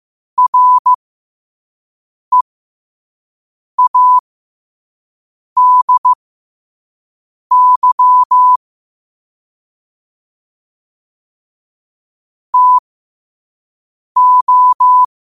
Morse code, heard as a keyed tone, about 15 wpm, spaced as for 5 wpm, 1000 Hz. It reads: READY TO
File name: qotdfiles / 5wpm